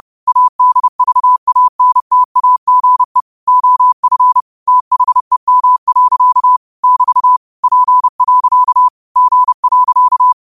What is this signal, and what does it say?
Morse code, heard as a keyed tone, about 30 wpm, 1000 Hz. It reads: ADVANTAGE OF THEM. <BT> P. G.
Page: New quotes every day in morse code at 30 Words per minute.